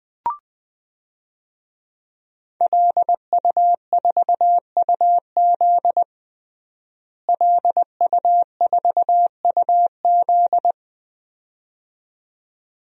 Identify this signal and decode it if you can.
Morse code, heard as a keyed tone, about 21 wpm, 700 Hz. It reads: LU4UZ LU4UZ
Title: DXCC-100 Call Signs - 5 Characters - Repeated 2x - 20wpm